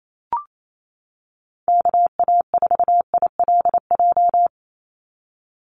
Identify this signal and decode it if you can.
Morse code, heard as a keyed tone, about 28 wpm, 700 Hz. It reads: KA4ILJ
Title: USA Call Signs 28wpm